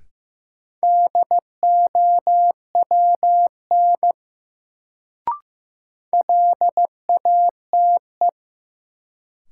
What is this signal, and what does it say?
Morse code, heard as a keyed tone, about 16 wpm, 700 Hz. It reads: DOWN LATE
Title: Sentences from Top 300 Words - New Word Drill - 15wpm